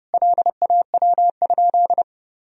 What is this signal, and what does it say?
Morse code, heard as a keyed tone, about 32 wpm, 700 Hz. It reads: LAW?
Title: Sentences from Top 1000 Words - Review Part 3 - 30wpm